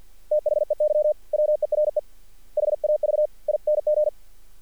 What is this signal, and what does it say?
Morse code, heard as a keyed tone, about 48 wpm, 600 Hz. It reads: THEY WERE HAVING
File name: archived / CRSnet